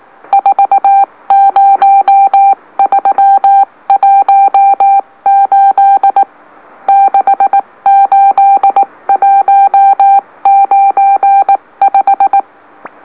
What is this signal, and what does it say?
Morse code, heard as a keyed tone, about 19 wpm, spaced as for 16 wpm, 790 Hz. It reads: 40318 68195
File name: M04- -2345utc-LOLO